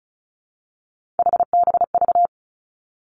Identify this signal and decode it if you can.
Morse code, heard as a keyed tone, about 36 wpm, 700 Hz. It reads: HBV
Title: Three Challenging Letter-Number Combinations 35wpm